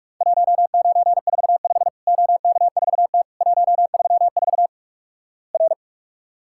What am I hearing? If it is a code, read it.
Morse code, heard as a keyed tone, about 46 wpm, 700 Hz. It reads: 1945 OK4T 134 R
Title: Worked All Europe DX Contest - QTC Exchange - 45wpm